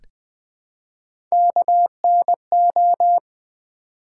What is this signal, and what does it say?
Morse code, heard as a keyed tone, about 20 wpm, 700 Hz. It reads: KNO
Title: Build A Word - 1 Letter At A Time - 4-Letter Words - 20wpm